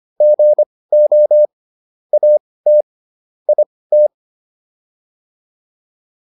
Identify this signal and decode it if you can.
Morse code, heard as a keyed tone, about 25 wpm, spaced as for 18 wpm, 600 Hz. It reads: GO AT IT